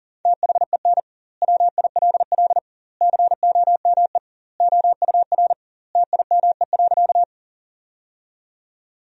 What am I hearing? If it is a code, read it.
Morse code, heard as a keyed tone, about 37 wpm, 700 Hz. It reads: THEN WILL COME OUR TIME.
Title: 2x Word-Spacing - Sentences from Top 200 Words - 40wpm